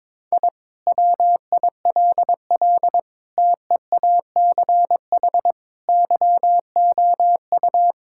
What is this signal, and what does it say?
Morse code, heard as a keyed tone, about 22 wpm, 700 Hz. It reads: I WILL TEACH YOU
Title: Sentences from Top 1000 Words - Review Part 6 - 22wpm